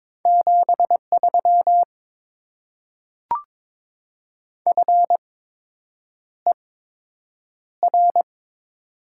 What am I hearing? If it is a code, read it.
Morse code, heard as a keyed tone, about 22 wpm, spaced as for 8 wpm, 700 Hz. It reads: 73 FER
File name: Top 20 QSO Elements - 8x Character Spacing - 22wpm